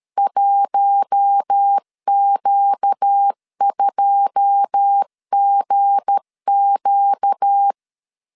Morse code, heard as a keyed tone, about 13 wpm, 800 Hz. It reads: 1Q2GQ
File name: CW / DET / 1 9 1500C S C 12wpm 3811